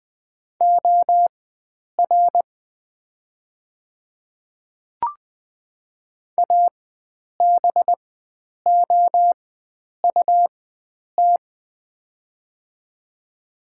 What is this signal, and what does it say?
Morse code, heard as a keyed tone, about 20 wpm, spaced as for 9 wpm, 700 Hz. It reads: OR ABOUT
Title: 4x Character Spacing - Top 100 Words - 20wpm